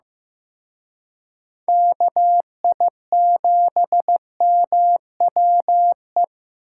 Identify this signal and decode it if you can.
Morse code, heard as a keyed tone, about 18 wpm, 700 Hz. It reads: KI7MWE
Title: Speed Racing - USA 2x3 Call Signs 15wpm